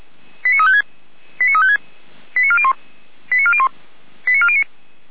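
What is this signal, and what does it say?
Five-tone selective call (ZVEI), heard as quick runs of five tones, tones of about 70 ms. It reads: 89356 89356 89401 89401 89409
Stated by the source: ZVEI five-tone call received on SDR, channel 455.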